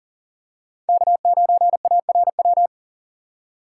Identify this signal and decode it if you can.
Morse code, heard as a keyed tone, about 40 wpm, 700 Hz. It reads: K9ARW
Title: USA Call Signs - 5 Characters - 2x Character Spacing - 40wpm